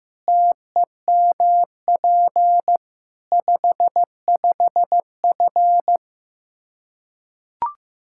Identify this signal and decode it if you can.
Morse code, heard as a keyed tone, about 15 wpm, 700 Hz. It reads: TEMP 55F
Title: Practice QSO - Exchange 2 - Info - 15wpm